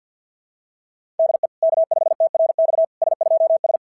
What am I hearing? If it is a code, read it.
Morse code, heard as a keyed tone, about 46 wpm, 690 Hz. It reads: DE K5TRX S2S
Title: SOTA QSO Practice 50wpm